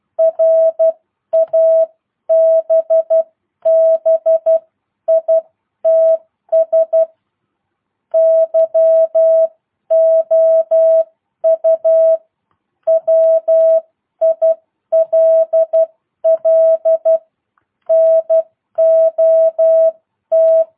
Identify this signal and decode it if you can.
Morse code, heard as a keyed tone, about 12 wpm, spaced as for 10 wpm, 650 Hz. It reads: RABBITS YOU WILL NOT